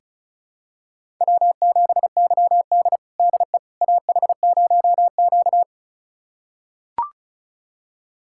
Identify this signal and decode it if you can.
Morse code, heard as a keyed tone, about 36 wpm, 700 Hz. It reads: W7YD DE AH0Q
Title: USA Callsign DE Callsign - 4 Characters - 35wpm